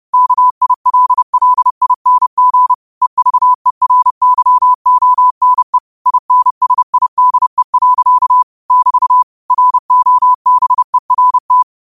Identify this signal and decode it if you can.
Morse code, heard as a keyed tone, about 30 wpm, 1000 Hz. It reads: KILLING EVERYONE INSIDE. <BT> ROBERT
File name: qotdfiles / 30wpm